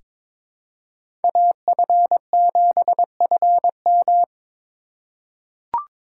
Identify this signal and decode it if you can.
Morse code, heard as a keyed tone, about 21 wpm, 700 Hz. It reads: AF7FM
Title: USA Call Signs - 5 Characters - 2x Character Spacing - 22wpm